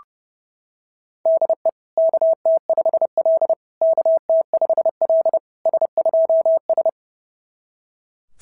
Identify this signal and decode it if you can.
Morse code, heard as a keyed tone, about 28 wpm, 660 Hz. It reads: DE KT5L KT5L S2S
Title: SOTA QSO Practice 30wpm